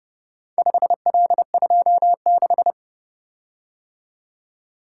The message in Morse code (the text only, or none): 5L26